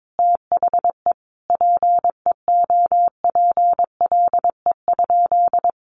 Morse code, heard as a keyed tone, about 22 wpm, 700 Hz. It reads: THE PEOPLE?
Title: Sentences from Top 1500 Words 22wpm